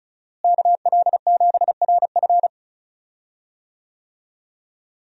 Morse code, heard as a keyed tone, about 36 wpm, 700 Hz. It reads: KL7RF
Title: DXCC-100 Call Signs - 5 Characters - Repeated 2x - 35wpm